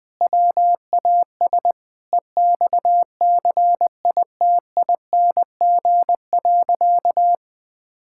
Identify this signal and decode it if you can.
Morse code, heard as a keyed tone, about 20 wpm, 700 Hz. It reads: WAS EXCITING.
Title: Sentences with 4-Word Phrases - 20wpm